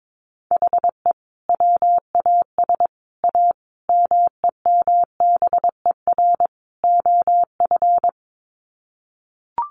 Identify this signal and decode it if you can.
Morse code, heard as a keyed tone, about 22 wpm, 700 Hz. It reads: HE WAS A MEMBER OF
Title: Sets of 5 Words Encore2 22wpm